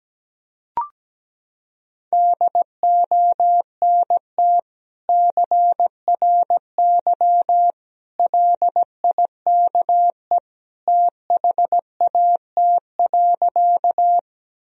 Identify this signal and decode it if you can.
Morse code, heard as a keyed tone, about 17 wpm, 700 Hz. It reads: DONT CRY LIKE THAT.
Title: Sentences from Top 1000 Words - Review Part 2 - 17wpm